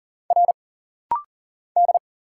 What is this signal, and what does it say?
Morse code, heard as a keyed tone, about 40 wpm, 700 Hz. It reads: R D
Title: Single Letter-Number Alt - Rapid-Fire 40wpm